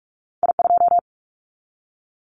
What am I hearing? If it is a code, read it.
Morse code, heard as a keyed tone, about 46 wpm, 700 Hz. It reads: I2
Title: Callsign Prefixes for Top 25 DXCC - 45wpm